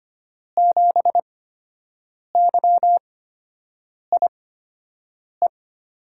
Morse code, heard as a keyed tone, about 25 wpm, spaced as for 7 wpm, 700 Hz. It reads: 7YIE
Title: USA Call Signs - 8x Character Spacing - 25wpm